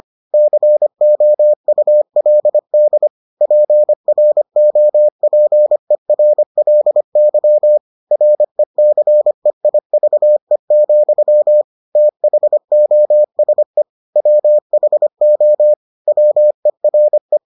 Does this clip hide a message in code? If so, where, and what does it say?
Yes: Morse code from 0.3 to 17.4 s, heard as a keyed tone, about 25 wpm, 600 Hz. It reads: COULD PROPERLY RECEIVE, THOSE WHO WERE